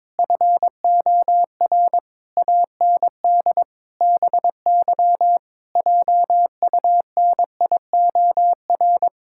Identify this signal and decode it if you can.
Morse code, heard as a keyed tone, about 22 wpm, 700 Hz. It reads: FOR AND BY JUNIOR
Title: Pangram Challenge v2 - 22wpm